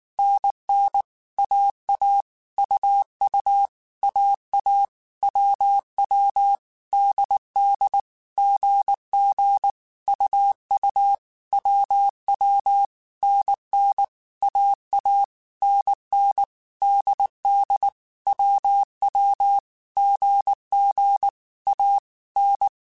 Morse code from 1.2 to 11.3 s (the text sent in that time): AA UU AA WW DD GG UU